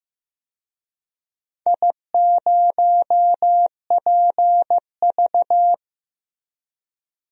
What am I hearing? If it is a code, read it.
Morse code, heard as a keyed tone, about 15 wpm, 700 Hz. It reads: I0PV
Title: DXCC-100 Call Signs - 4 Characters - 15wpm